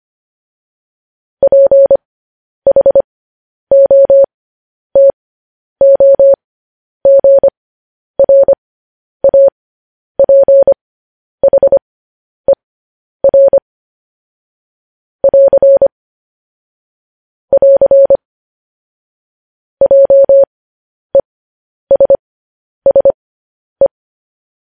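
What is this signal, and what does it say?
Morse code, heard as a keyed tone, about 25 wpm, spaced as for 10 wpm, 550 Hz. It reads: PHOTOGRAPHER <AR> <AR> JESSE
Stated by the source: Morse code news headlines broadcast, opening with vvv.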